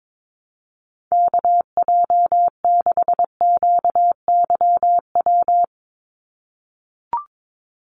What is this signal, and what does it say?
Morse code, heard as a keyed tone, about 22 wpm, 700 Hz. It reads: KJ6QYW
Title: USA Call Signs 22wpm